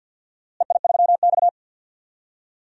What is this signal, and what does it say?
Morse code, heard as a keyed tone, about 53 wpm, 700 Hz. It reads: EI3X
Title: DXCC-100 Call Signs - 4 Characters - Repeated 2x - 50wpm